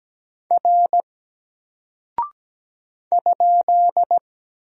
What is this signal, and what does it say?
Morse code, heard as a keyed tone, about 17 wpm, 700 Hz. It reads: R ?
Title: Single Character (Letter:Number:Punctuation:Prosign) 17wpm